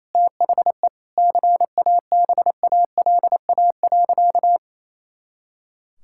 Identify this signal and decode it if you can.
Morse code, heard as a keyed tone, about 28 wpm, 700 Hz. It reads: THE CABALA.
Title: Pangram Challenge v2 - 28wpm